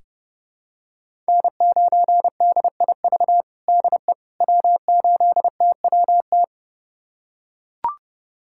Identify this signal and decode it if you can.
Morse code, heard as a keyed tone, about 30 wpm, 700 Hz. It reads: N9DIV DE W8TWT
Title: USA Callsign DE Callsign - 5 Characters - 30wpm